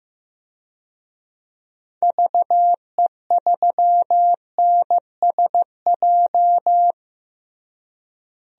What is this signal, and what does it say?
Morse code, heard as a keyed tone, about 15 wpm, 700 Hz. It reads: VE3NSJ
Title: DXCC-100 Call Signs - 6 Characters - Repeated 3x - 15wpm